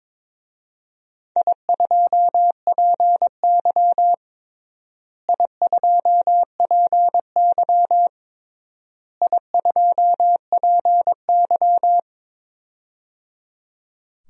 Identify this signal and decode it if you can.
Morse code, heard as a keyed tone, about 22 wpm, 700 Hz. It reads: I2PY I2PY I2PY
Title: DXCC-100 Call Signs - 4 Characters - Repeated 3x - 22wpm